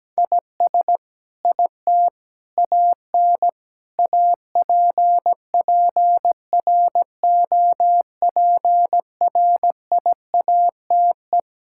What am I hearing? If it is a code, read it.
Morse code, heard as a keyed tone, about 17 wpm, 700 Hz. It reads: IS IT AN APPROPRIATE
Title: Sentences from Top 1000 Words - Review Part 5 - 17wpm